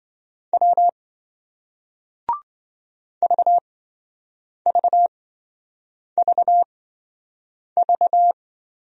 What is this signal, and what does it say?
Morse code, heard as a keyed tone, about 28 wpm, spaced as for 7 wpm, 700 Hz. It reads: W VVVV